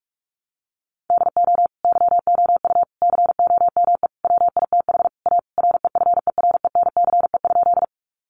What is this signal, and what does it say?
Morse code, heard as a keyed tone, about 44 wpm, 700 Hz. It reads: DO YOU COME WITH A REFERENCE?
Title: Sentences from Top 1200 Words 45wpm